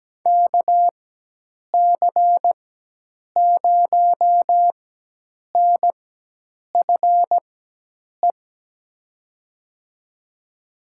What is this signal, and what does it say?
Morse code, heard as a keyed tone, about 17 wpm, spaced as for 8 wpm, 700 Hz. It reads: KC0NFE
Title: USA Call Signs - 4x Character Spacing - 17wpm